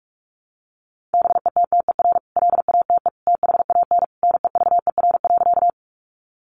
Morse code, heard as a keyed tone, about 44 wpm, 700 Hz. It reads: BETTER LATE THAN NEVER.
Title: Sentences from Top 400 Words 45wpm